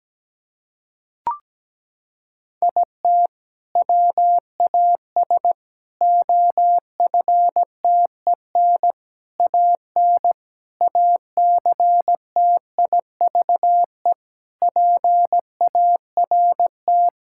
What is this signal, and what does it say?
Morse code, heard as a keyed tone, about 17 wpm, 700 Hz. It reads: IT WAS OFTEN AN ACTIVE PART
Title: Sentences from Top 1000 Words - Review Part 6 - 17wpm